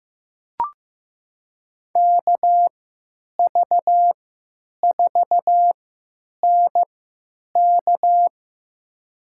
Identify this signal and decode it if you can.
Morse code, heard as a keyed tone, about 15 wpm, spaced as for 9 wpm, 700 Hz. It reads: KV4NK
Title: USA Call Signs - 5 Characters - 3x Character Spacing - 15wpm